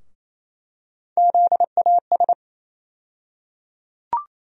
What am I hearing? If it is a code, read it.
Morse code, heard as a keyed tone, about 28 wpm, 700 Hz. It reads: ZAS
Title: USA Call Sign Postfixes - 3 Letters 28wpm